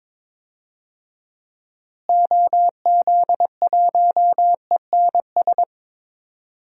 Morse code, heard as a keyed tone, about 22 wpm, 700 Hz. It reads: OZ1ENS
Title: DXCC-100 Call Signs - 6 Characters - Repeated 2x - 22wpm